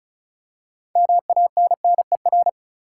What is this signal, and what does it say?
Morse code, heard as a keyed tone, about 36 wpm, 700 Hz. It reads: MANNER
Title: Six Letter Words - 35wpm